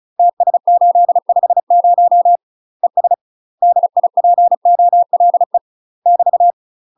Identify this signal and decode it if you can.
Morse code, heard as a keyed tone, about 36 wpm, 700 Hz. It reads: TS850 ES DIPOLE <BT>
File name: UK MASTER QSOs TRAINING / 35WPM / UK MASTER QSOs 0014